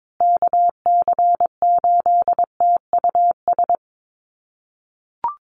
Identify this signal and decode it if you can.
Morse code, heard as a keyed tone, about 22 wpm, 700 Hz. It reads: KC8TUS
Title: USA Call Signs Repeated 2x 22wpm